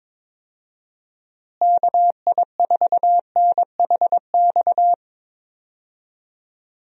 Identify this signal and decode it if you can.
Morse code, heard as a keyed tone, about 21 wpm, 700 Hz. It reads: KI4NHX
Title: USA Call Signs - 6 Characters - Repeated 2x - 22wpm